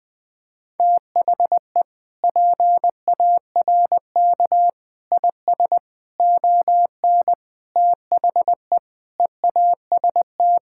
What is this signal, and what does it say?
Morse code, heard as a keyed tone, about 20 wpm, 700 Hz. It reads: THE PARK IS ON THE EAST
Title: Sentences with 6-Word Phrases - 20wpm